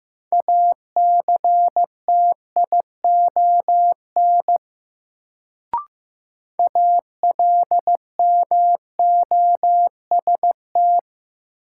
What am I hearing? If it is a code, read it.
Morse code, heard as a keyed tone, about 15 wpm, 700 Hz. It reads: ACTION ALMOST
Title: Sentences from Top 700 Words - New Word Drill - 15wpm